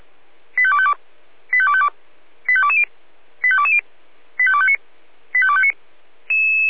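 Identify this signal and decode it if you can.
Five-tone selective call (ZVEI), heard as quick runs of five tones, tones of about 70 ms. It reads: 85252 85252 85220 85220 85259 85259
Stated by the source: Received on an SDR; ZVEI five-tone selective call, code 85295.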